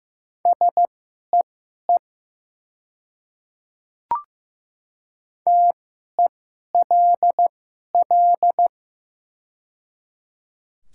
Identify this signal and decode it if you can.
Morse code, heard as a keyed tone, about 15 wpm, spaced as for 11 wpm, 700 Hz. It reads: SEE TELL
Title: 2x Character Spacing - Top 500 Words - 15wpm